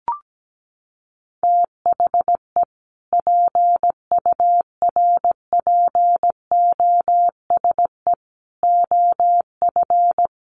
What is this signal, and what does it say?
Morse code, heard as a keyed tone, about 17 wpm, 700 Hz. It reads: THE PURPOSE OF
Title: Sets of 5 Words 17wpm